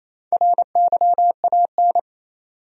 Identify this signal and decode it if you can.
Morse code, heard as a keyed tone, about 28 wpm, 700 Hz. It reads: RYAN
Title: USA Common Names - Max 4 Letters - 28wpm